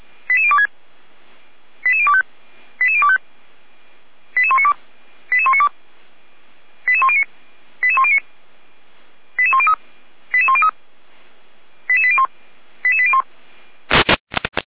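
Five-tone selective call (ZVEI), heard as quick runs of five tones, tones of about 70 ms. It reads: 88026 88025 88025 88192 88192 88109 88109 88103 88103 88801 88801
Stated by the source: ZVEI five-tone call 88102 received by SDR, from feuerwehr-orlamuende.org (fire brigade Orlamünde).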